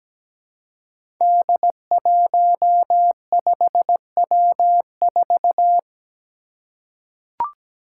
Morse code, heard as a keyed tone, about 17 wpm, 700 Hz. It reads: D15W4
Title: Five Challenging Letter-Number Combinations 17wpm